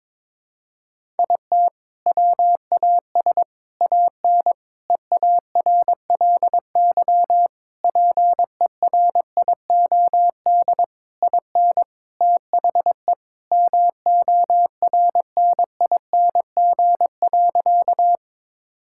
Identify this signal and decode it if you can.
Morse code, heard as a keyed tone, about 22 wpm, 700 Hz. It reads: IT WAS AN EARLY PERIOD IN THE MORNING.
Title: Sentences from Top 1000 Words - Review Part 5 - 22wpm